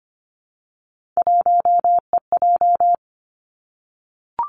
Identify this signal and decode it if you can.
Morse code, heard as a keyed tone, about 25 wpm, 700 Hz. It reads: 1EJ
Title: Three Challenging Letter-Number Combinations 25wpm